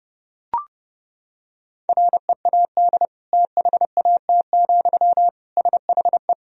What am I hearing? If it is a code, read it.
Morse code, heard as a keyed tone, about 30 wpm, 700 Hz. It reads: READ THAT, SHE
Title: Sentences from Top 200 Words and Abbreviations 30wpm